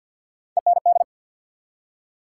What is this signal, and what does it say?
Morse code, heard as a keyed tone, about 53 wpm, 700 Hz. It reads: END
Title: Three Letter Words - 50wpm